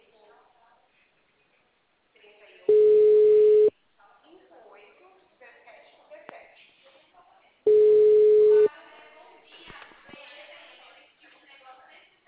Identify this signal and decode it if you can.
Ringback tone (Europe, 425 Hz), with speech.
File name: out- -2007- - - .45409